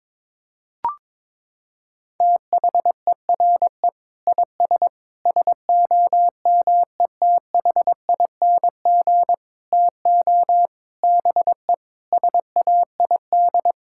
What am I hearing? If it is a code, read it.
Morse code, heard as a keyed tone, about 22 wpm, 700 Hz. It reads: THERE IS SOMETHING TO BE SAID
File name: Sentences with 7-Word Phrases - 22wpm